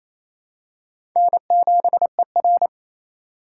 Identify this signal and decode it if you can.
Morse code, heard as a keyed tone, about 28 wpm, 700 Hz. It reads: N7ER
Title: USA Call Signs - 4 Characters - 6x Character Spacing - 28wpm